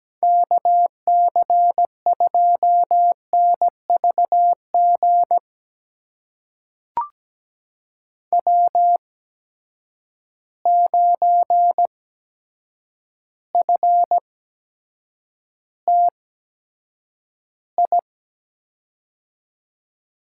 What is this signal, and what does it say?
Morse code, heard as a keyed tone, about 17 wpm, spaced as for 8 wpm, 700 Hz. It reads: KC2NVG W9FTI
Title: USA Call Signs - 8x Character Spacing - 17wpm